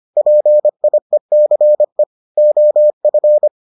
Morse code, heard as a keyed tone, about 25 wpm, 600 Hz. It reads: PIECE OF